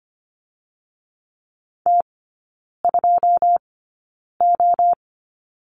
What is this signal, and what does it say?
Morse code, heard as a keyed tone, about 25 wpm, spaced as for 9 wpm, 690 Hz. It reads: T2O
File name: Lesson 14e - Number 2 - Three Characters 25wpm